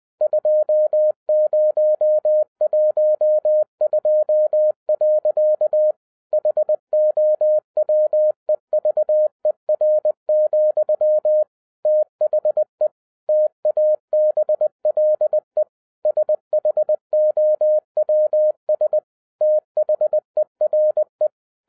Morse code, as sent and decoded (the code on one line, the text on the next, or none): ..--- ----- .---- ..--- .-.-.- / .... --- .-- . ...- . .-. --..-- / - .... / - .- -... .-.. / ... .... --- .-- ... / - .... .-. .
2012. HOWEVER, THE TABLE SHOWS THERE